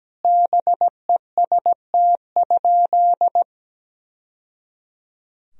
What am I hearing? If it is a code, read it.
Morse code, heard as a keyed tone, about 17 wpm, 700 Hz. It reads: BEST?
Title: Sentences from Top 1000 Words - Review Part 2 - 17wpm